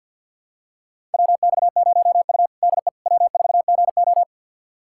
Morse code, heard as a keyed tone, about 50 wpm, 700 Hz. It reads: WX0U DE W4GO
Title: USA Callsign DE Callsign - 4 Characters - 50wpm